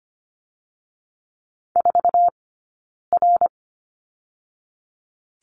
Morse code, heard as a keyed tone, about 25 wpm, 700 Hz. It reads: 4 R
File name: Lesson 15d - Number 5 - Two Characters 25wpm